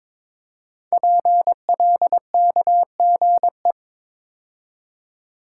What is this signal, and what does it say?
Morse code, heard as a keyed tone, about 22 wpm, 700 Hz. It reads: PLKGE
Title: Code Groups - 5 Letters - 22wpm